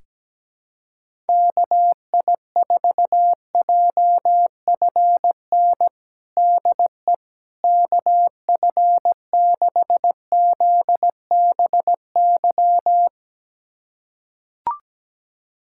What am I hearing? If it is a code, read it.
Morse code, heard as a keyed tone, about 17 wpm, 700 Hz. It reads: KI4JFN DE KF6ZBY
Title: USA Callsign DE Callsign - 6 Characters - 17wpm